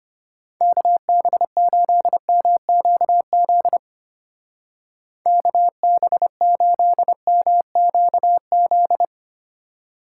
Morse code, heard as a keyed tone, about 31 wpm, 700 Hz. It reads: KB8MQZ KB8MQZ
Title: Speed Racing - USA 2x3 Call Signs 25wpm